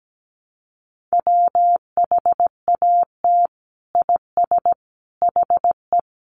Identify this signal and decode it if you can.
Morse code, heard as a keyed tone, about 17 wpm, 700 Hz. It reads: WHAT IS HE
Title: Sentences from Top 100 Words 17wpm